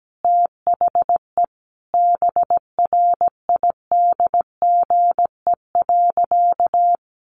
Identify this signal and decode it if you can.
Morse code, heard as a keyed tone, about 17 wpm, 700 Hz. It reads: THE BRIDGE.
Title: Sentences from Top 1200 Words 17wpm